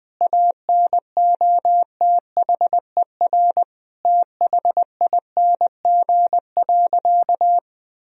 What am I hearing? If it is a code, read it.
Morse code, heard as a keyed tone, about 20 wpm, 700 Hz. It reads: ANOTHER THING.
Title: Sentences from Top 1000 Words - Review Part 3 - 20wpm